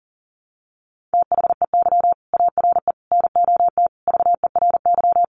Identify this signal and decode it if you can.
Morse code, heard as a keyed tone, about 37 wpm, 700 Hz. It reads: THEY ARE NOT VERY
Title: Sentences from Top 1400 Words 40wpm